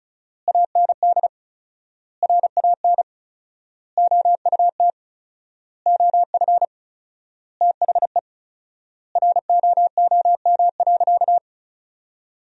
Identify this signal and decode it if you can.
Morse code, heard as a keyed tone, about 34 wpm, 700 Hz. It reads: AND RAN OUT OF THE ROOM.
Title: 4x Word-Spacing - Sentences from Top 500 Words - 35wpm